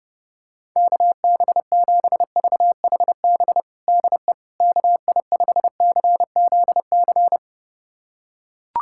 Morse code, heard as a keyed tone, about 31 wpm, 700 Hz. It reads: KB7VHB DE KI5CZC
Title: USA Callsign DE Callsign - 6 Characters - 30wpm